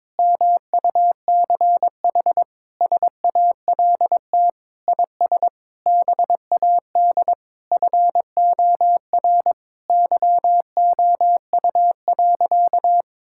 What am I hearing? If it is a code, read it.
Morse code, heard as a keyed tone, about 22 wpm, 700 Hz. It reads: MUCH SALT IS BAD FOR YOU.
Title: Sentences from Top 1000 Words - Review Part 5 - 22wpm